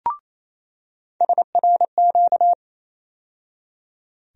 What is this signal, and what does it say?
Morse code, heard as a keyed tone, about 28 wpm, 700 Hz. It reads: SRQ
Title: USA Call Sign Postfixes - 3 Letters 28wpm